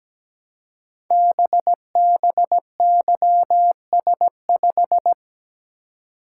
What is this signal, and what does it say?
Morse code, heard as a keyed tone, about 17 wpm, 700 Hz. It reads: BBYS5
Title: Code Groups - 5 Letter:Numbers - 17wpm